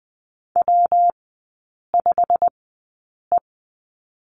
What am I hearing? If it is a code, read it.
Morse code, heard as a keyed tone, about 20 wpm, spaced as for 8 wpm, 700 Hz. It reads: W5E
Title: 114 Lesson 19e - Letter w - Three Characters 20wpm